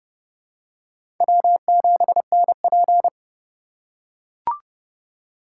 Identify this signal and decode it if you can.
Morse code, heard as a keyed tone, about 28 wpm, 700 Hz. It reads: W7NP